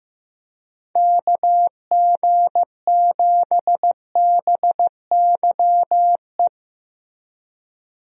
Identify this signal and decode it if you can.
Morse code, heard as a keyed tone, about 15 wpm, 700 Hz. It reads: KG7BYE
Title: USA Call Signs - 6 Characters - Repeated 2x - 15wpm